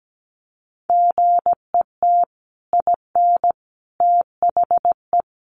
Morse code, heard as a keyed tone, about 17 wpm, 700 Hz. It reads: GET IN THE
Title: Sets of 5 Words Encore 17wpm